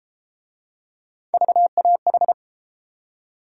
Morse code, heard as a keyed tone, about 31 wpm, 700 Hz. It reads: VAH